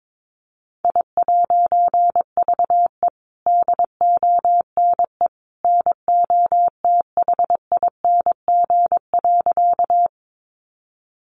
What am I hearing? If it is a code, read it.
Morse code, heard as a keyed tone, about 22 wpm, 700 Hz. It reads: I'VE DONE NOTHING.
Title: A Sound Of Thunder 22wpm